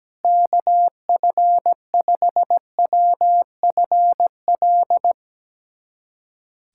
Morse code, heard as a keyed tone, about 17 wpm, 700 Hz. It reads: KF5WFL
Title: USA Callsign DE Callsign 17wpm